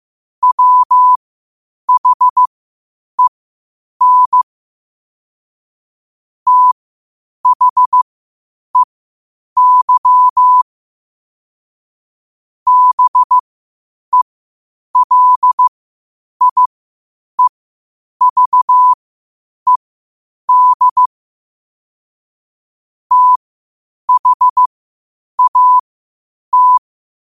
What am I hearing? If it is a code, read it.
Morse code, heard as a keyed tone, about 15 wpm, spaced as for 8 wpm, 1000 Hz. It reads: WHEN THEY BELIEVED THAT
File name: qotdfiles / 8wpm